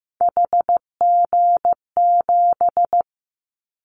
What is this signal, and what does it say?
Morse code, heard as a keyed tone, about 15 wpm, 700 Hz. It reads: HG7
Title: Callsign Prefixes for Top 50 DXCC - 15wpm